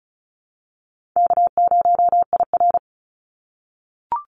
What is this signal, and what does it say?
Morse code, heard as a keyed tone, about 34 wpm, 700 Hz. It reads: K0IR
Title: Call Signs - 35wpm